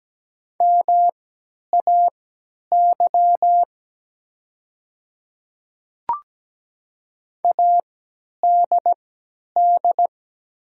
Morse code, heard as a keyed tone, about 17 wpm, spaced as for 10 wpm, 700 Hz. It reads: MAY ADD
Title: 3x Character Spacing - Top 300 Words - 17wpm